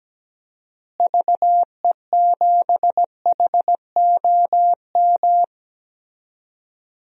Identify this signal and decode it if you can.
Morse code, heard as a keyed tone, about 17 wpm, 700 Hz. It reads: VE7HOM
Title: DXCC-100 Call Signs - 6 Characters - Repeated 3x - 17wpm